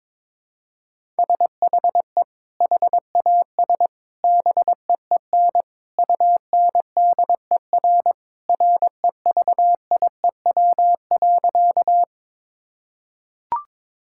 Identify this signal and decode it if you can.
Morse code, heard as a keyed tone, about 22 wpm, 700 Hz. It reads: SHE HAS BEEN UNDER REVIEW.
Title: Sentences from Top 1000 Words - Review Part 6 - 22wpm